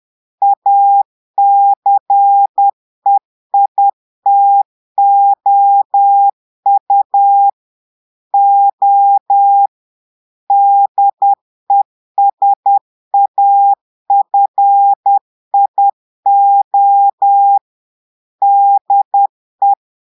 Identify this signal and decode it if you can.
Morse code, heard as a keyed tone, about 10 wpm, 800 Hz. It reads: ACEITOU O DESAFIO DE
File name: Portugais - texte en clair 1 - vitesse - 10 mots minute